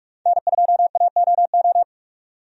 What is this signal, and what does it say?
Morse code, heard as a keyed tone, about 42 wpm, 700 Hz. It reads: N2AOO
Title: USA Call Signs - 5 Characters - 45wpm